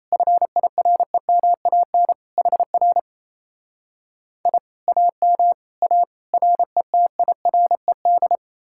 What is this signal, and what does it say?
Morse code, heard as a keyed tone, about 37 wpm, 700 Hz. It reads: FIREMAN HR I AM A RETIRED